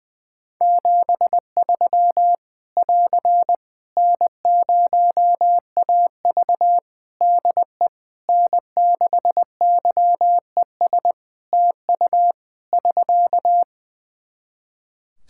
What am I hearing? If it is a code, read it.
Morse code, heard as a keyed tone, about 19 wpm, 700 Hz. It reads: 73 <AR> N0AV DE N6YES TU <SK>
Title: Practice QSO - Exchange 4 - Ending - 20wpm